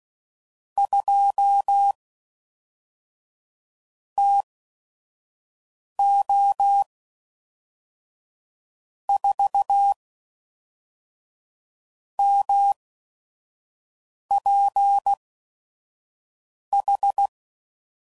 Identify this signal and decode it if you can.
Morse code, heard as a keyed tone, about 16 wpm, spaced as for 5 wpm, 800 Hz. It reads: 2 TO 4 MPH
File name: CW Code Groups @5WPM / Weather Sets / Weather 4 14